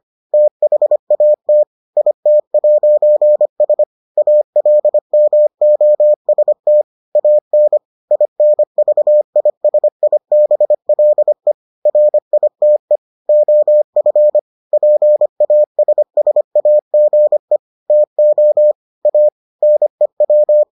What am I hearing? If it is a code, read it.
Morse code, heard as a keyed tone, about 25 wpm, 600 Hz. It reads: THAT IT'S ALMOST AN INVISIBLE RITE OF PASSAGE TO A NEW